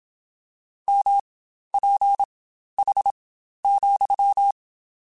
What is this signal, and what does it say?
Morse code, heard as a keyed tone, about 27 wpm, spaced as for 12 wpm, 800 Hz. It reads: MPH,
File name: Spaced Characters / Spaced Characters-mp3s / Weather4 21 sp @25